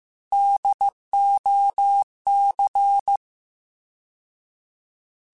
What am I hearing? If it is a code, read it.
Morse code, heard as a keyed tone, about 15 wpm, 800 Hz. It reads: DOC
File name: Alphabetical / Alphabetical mp3s / Ds @13